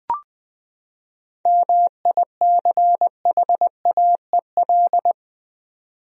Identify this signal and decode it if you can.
Morse code, heard as a keyed tone, about 20 wpm, 700 Hz. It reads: MICHAEL
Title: USA Common Names - 20wpm